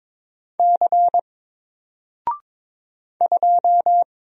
Morse code, heard as a keyed tone, about 22 wpm, 700 Hz. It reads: C 2
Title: Single Letter-Number Alt 22wpm